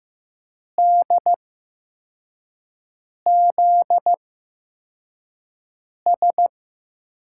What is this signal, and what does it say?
Morse code, heard as a keyed tone, about 15 wpm, spaced as for 4 wpm, 700 Hz. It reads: DZS
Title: USA Call Signs - 6 Characters - 8x Character Spacing - 15wpm